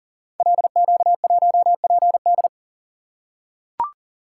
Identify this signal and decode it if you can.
Morse code, heard as a keyed tone, about 42 wpm, 700 Hz. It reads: LQ1PD
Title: Code Groups - 5 Letter:Numbers - 40wpm